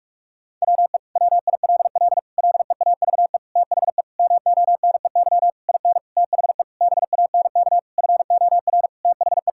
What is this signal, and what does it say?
Morse code, heard as a keyed tone, about 41 wpm, 700 Hz. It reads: WE WILL LEAVE THE MONEY IN THE BANK FOR THE
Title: Sentences from Top 1000 Words - Review Part 6 - 45wpm